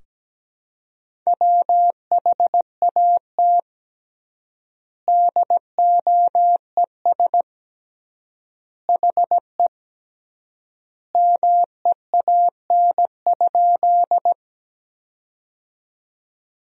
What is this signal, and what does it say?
Morse code, heard as a keyed tone, about 17 wpm, 700 Hz. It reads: WHAT DOES HE MEAN?
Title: 3x Word-Spacing - Sentences from Top 200 Words - 17wpm